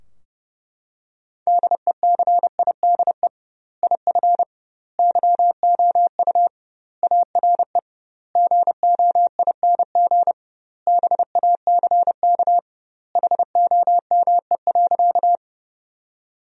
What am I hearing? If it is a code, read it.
Morse code, heard as a keyed tone, about 29 wpm, 700 Hz. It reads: DECIDE IF YOU ARE GOING BACK HOME.
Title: 2x Word-Spacing - Sentences from Top 500 Words - 30wpm